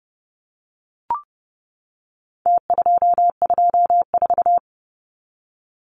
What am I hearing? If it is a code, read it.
Morse code, heard as a keyed tone, about 30 wpm, 700 Hz. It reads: T224
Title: Numbers - Contest Style - 4-Digit 30wpm